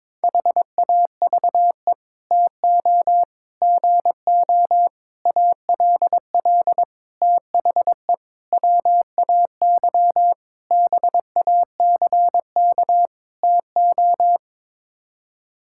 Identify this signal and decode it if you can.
Morse code, heard as a keyed tone, about 22 wpm, 700 Hz. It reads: HAVE TO GO ALL THE WAY BACK TO
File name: practice / 22wpm / Sets of 8 Words - 22wpm